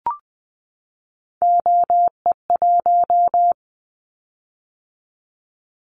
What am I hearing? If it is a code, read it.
Morse code, heard as a keyed tone, about 20 wpm, 700 Hz. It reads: OE1
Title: Callsign Prefixes for Top 100 DXCC - 20wpm